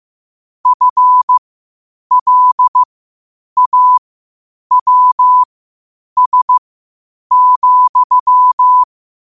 Morse code, heard as a keyed tone, about 15 wpm, spaced as for 8 wpm, 1000 Hz. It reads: FLAWS,
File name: qotdfiles / 8wpm